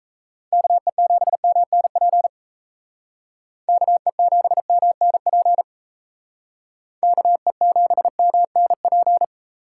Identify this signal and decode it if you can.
Morse code, heard as a keyed tone, about 36 wpm, 700 Hz. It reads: KE7MNP KE7MNP KE7MNP